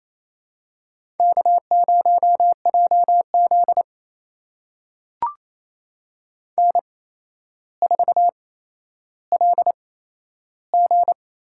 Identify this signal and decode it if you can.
Morse code, heard as a keyed tone, about 28 wpm, spaced as for 12 wpm, 700 Hz. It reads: K0JZ N4LG
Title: USA Call Signs - 4 Characters - 8x Character Spacing - 28wpm